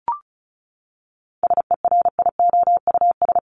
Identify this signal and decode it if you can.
Morse code, heard as a keyed tone, about 34 wpm, 700 Hz. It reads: SERIOUS
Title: Top 1001-1250 Words 35wpm